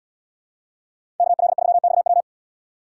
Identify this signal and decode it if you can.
Morse code, heard as a keyed tone, about 70 wpm, 700 Hz. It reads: KL3XW